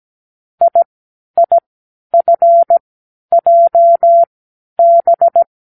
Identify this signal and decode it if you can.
Morse code, heard as a keyed tone, about 17 wpm, spaced as for 10 wpm, 680 Hz. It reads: IIFJB
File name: k7qo / Lesson15(A-J)